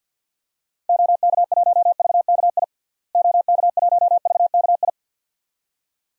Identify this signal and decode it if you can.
Morse code, heard as a keyed tone, about 51 wpm, 700 Hz. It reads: OK1VKI OK1VKI
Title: DXCC-100 Call Signs - 6 Characters - Repeated 2x - 50wpm